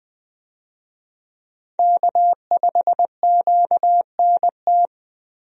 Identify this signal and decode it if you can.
Morse code, heard as a keyed tone, about 20 wpm, 700 Hz. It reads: K5QNT
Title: USA Call Signs - 5 Characters - Repeated 2x - 20wpm